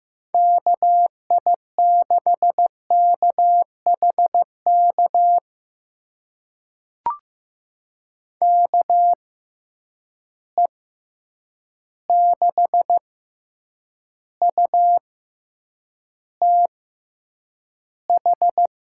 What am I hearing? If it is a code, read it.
Morse code, heard as a keyed tone, about 15 wpm, spaced as for 8 wpm, 700 Hz. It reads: KI6KHK KE6UTH
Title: USA Call Signs - 6 Characters - 6x Character Spacing - 15wpm